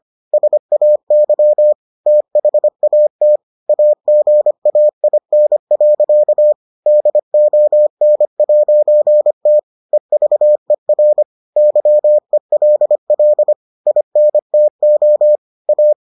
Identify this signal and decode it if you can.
Morse code, heard as a keyed tone, about 25 wpm, 600 Hz. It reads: SAY THAT AGAIN. DON'T EVER YELL INTO A